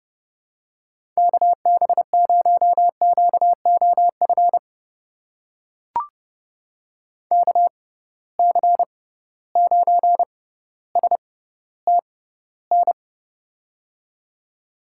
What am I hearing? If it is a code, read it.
Morse code, heard as a keyed tone, about 30 wpm, spaced as for 15 wpm, 700 Hz. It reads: KB0QOF KC9STN
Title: USA Call Signs - 6x Character Spacing - 30wpm